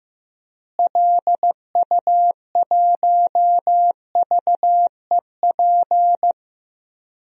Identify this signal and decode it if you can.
Morse code, heard as a keyed tone, about 15 wpm, 700 Hz. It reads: LU1VEP
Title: DXCC-100 Call Signs - 6 Characters - Repeated 2x - 15wpm